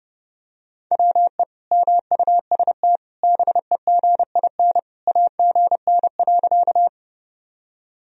Morse code, heard as a keyed tone, about 30 wpm, 700 Hz. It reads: WE MUST BEGIN AGN.
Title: Sentences from Top 300 Words and Abbreviations 30wpm